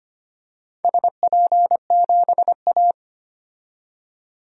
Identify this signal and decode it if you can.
Morse code, heard as a keyed tone, about 25 wpm, 700 Hz. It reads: SP7A
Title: DXCC-100 Call Signs - 4 Characters - 25wpm